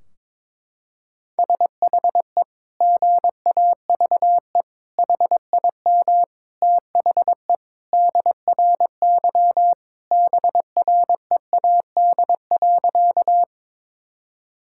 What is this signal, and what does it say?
Morse code, heard as a keyed tone, about 21 wpm, 700 Hz. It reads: SHE GAVE HIM THE DRY BREAD.
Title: Sentences from Top 1600 Words 22wpm